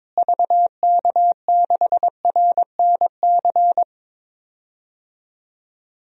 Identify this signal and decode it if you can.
Morse code, heard as a keyed tone, about 21 wpm, 700 Hz. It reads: VK6RNC
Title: DXCC-100 Call Signs - 6 Characters - Repeated 3x - 22wpm